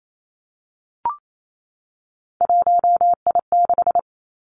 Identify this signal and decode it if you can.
Morse code, heard as a keyed tone, about 28 wpm, 700 Hz. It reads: 1I6
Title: Three Challenging Letter-Number Combinations 28wpm